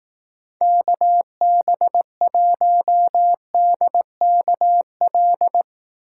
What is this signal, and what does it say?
Morse code, heard as a keyed tone, about 18 wpm, 700 Hz. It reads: KB1DKL
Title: Speed Racing - USA 2x3 Call Signs 15wpm